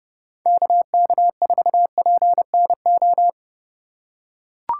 Morse code, heard as a keyed tone, about 30 wpm, 700 Hz. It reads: KK4PNO
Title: USA Call Signs - 6 Characters - 3x Character Spacing - 30wpm